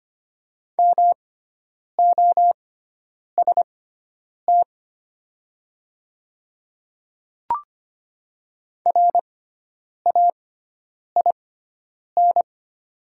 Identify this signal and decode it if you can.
Morse code, heard as a keyed tone, about 25 wpm, spaced as for 9 wpm, 700 Hz. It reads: MOST RAIN
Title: 6x Character Spacing - Top 500 Words - 25wpm